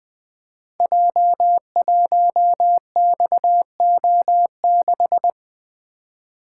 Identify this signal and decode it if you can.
Morse code, heard as a keyed tone, about 20 wpm, 700 Hz. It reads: J1XO6
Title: Code Groups - 5 Letter:Numbers - 20wpm